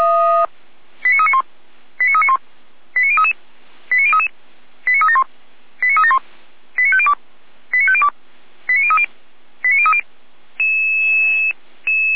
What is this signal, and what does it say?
Five-tone selective call (ZVEI), heard as quick runs of five tones, tones of about 70 ms. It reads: 89391 89391 89033 89033 89371 89371 89502 89502 89030 89030